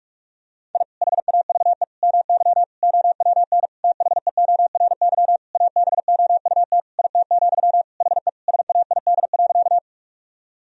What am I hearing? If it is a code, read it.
Morse code, heard as a keyed tone, about 46 wpm, 700 Hz. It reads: I HAVE MY OWN THEORY ABOUT IT, HE SAID.
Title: Sentences from Top 1000 Words - Review Part 3 - 45wpm